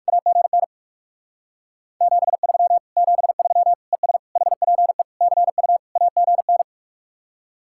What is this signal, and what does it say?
Morse code, heard as a keyed tone, about 56 wpm, 700 Hz. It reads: AGN 73 73 ES HPE CU AGN
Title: Speed-Racing Common QSO Phrases - 45wpm